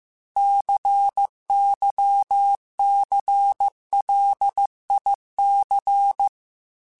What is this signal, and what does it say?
Morse code, heard as a keyed tone, about 15 wpm, 800 Hz. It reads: CYCLIC